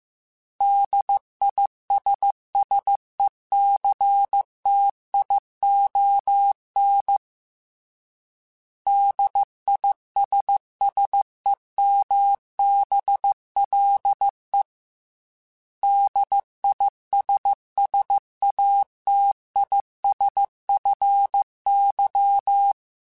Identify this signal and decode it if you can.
Morse code, heard as a keyed tone, about 15 wpm, 800 Hz. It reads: DISSECTION DISSEMBLE DISSATISFY